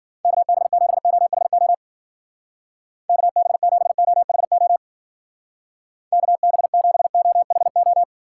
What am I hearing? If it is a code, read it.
Morse code, heard as a keyed tone, about 50 wpm, 700 Hz. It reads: KB7OHO KB7OHO KB7OHO
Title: Speed Racing - USA 2x3 Call Signs 40wpm